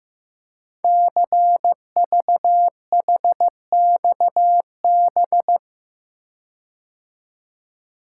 Morse code, heard as a keyed tone, about 15 wpm, 700 Hz. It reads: CVHXB
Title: Code Groups - 5 Letters - 15wpm